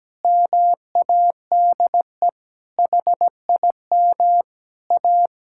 Morse code, heard as a keyed tone, about 17 wpm, 700 Hz. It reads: MADE HIM A
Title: Sentences from Top 1000 Words - Review Part 5 - 17wpm